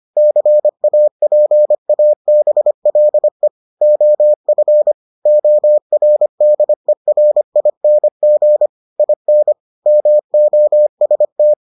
Morse code, heard as a keyed tone, about 25 wpm, 600 Hz. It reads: CAPABLE OF ORDERING IN MOST